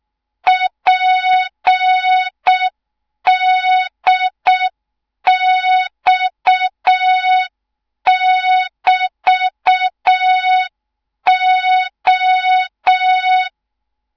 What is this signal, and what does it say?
Morse code, heard as a keyed tone, about 6 wpm, 3700 Hz. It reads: PDX<BT>O
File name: Morze-HA2MN / hang / 24 GYAKORLAT